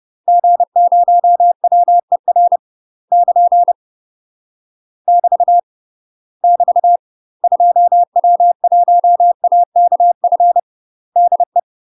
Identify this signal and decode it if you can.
Morse code, heard as a keyed tone, about 30 wpm, 700 Hz. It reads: G0WER <KN> <BT> <BT> 2W1AKF DE
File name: UK MASTER QSOs TRAINING/30WPM